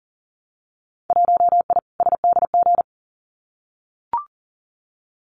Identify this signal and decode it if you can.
Morse code, heard as a keyed tone, about 40 wpm, 700 Hz. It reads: 1I SDG
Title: Winter Field Day Exchanges 40wpm